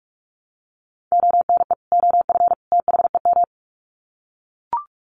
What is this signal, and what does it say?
Morse code, heard as a keyed tone, about 44 wpm, 700 Hz. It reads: ONE OF THEM
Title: Sets of 3 Words 45wpm